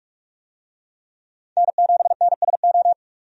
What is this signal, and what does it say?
Morse code, heard as a keyed tone, about 46 wpm, 700 Hz. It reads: N7NSO
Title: USA Call Signs - 5 Characters - 45wpm